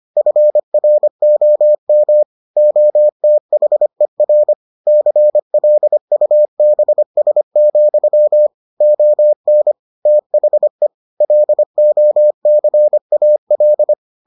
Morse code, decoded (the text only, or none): FROM OTHER CLUBS, ON THE LOCAL